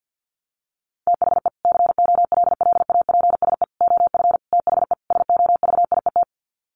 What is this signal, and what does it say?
Morse code, heard as a keyed tone, about 50 wpm, 700 Hz. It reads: THE COLLAPSE OF THE SOVIET
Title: Sets of 5 Words Encore 50wpm